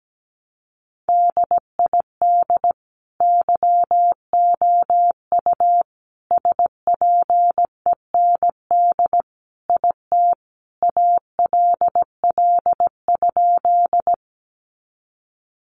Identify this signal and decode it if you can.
Morse code, heard as a keyed tone, about 17 wpm, 700 Hz. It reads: DID YOU SPEND IT ALL?
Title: Sentences from Top 800 Words 17wpm